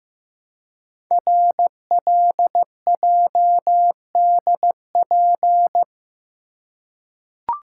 Morse code, heard as a keyed tone, about 15 wpm, 700 Hz. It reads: RLJDP
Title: Code Groups - 5 Letters - 15wpm